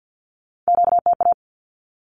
Morse code, heard as a keyed tone, about 50 wpm, 700 Hz. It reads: QTA
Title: Q-Codes 50wpm